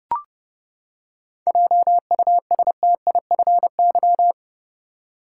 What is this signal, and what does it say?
Morse code, heard as a keyed tone, about 32 wpm, 700 Hz. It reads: JUSTIFY
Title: Seven Letter Words - 30wpm